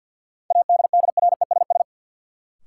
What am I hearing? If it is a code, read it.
Morse code, heard as a keyed tone, about 52 wpm, 700 Hz. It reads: ADDRESS
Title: Sets of 5 Words Encore - New Word Drill - 50wpm